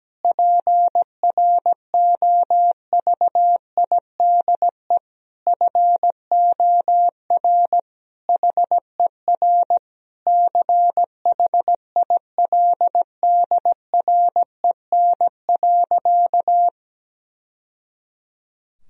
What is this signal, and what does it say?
Morse code, heard as a keyed tone, about 17 wpm, 700 Hz. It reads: PROVIDE FOR HER CHILDREN.
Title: Sentences from Top 1000 Words - Review Part 3 - 17wpm